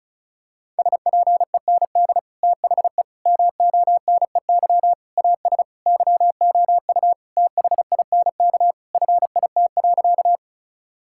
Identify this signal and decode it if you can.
Morse code, heard as a keyed tone, about 34 wpm, 700 Hz. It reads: SPEND THE MONEY AS YOU THINK FIT.
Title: Sentences from Top 1000 Words - Review Part 5 - 35wpm